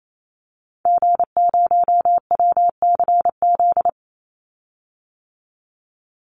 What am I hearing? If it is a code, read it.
Morse code, heard as a keyed tone, about 28 wpm, 700 Hz. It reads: G0WCZ
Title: Call Signs - 28wpm